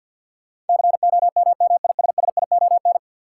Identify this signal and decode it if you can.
Morse code, heard as a keyed tone, about 50 wpm, 700 Hz. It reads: COMMISSION
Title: Sentences from Top 1400 Words - New Word Drill Alt - 50wpm